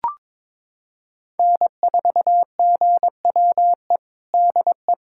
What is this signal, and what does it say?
Morse code, heard as a keyed tone, about 22 wpm, 700 Hz. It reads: N4GWE DE
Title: USA Callsign DE Callsign 22wpm